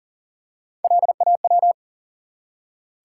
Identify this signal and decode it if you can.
Morse code, heard as a keyed tone, about 40 wpm, 700 Hz. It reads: LAW
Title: Three Letter Words - 40wpm